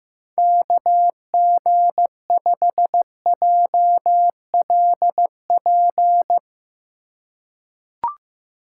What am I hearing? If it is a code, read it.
Morse code, heard as a keyed tone, about 15 wpm, 700 Hz. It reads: KG5JLP
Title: USA Call Signs - 6 Characters - 4x Character Spacing - 15wpm